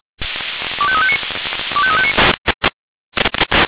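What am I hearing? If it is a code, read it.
Five-tone selective call (ZVEI), heard as quick runs of five tones, tones of about 70 ms. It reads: 25359 25359
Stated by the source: ZVEI five-tone selective call received on an SDR.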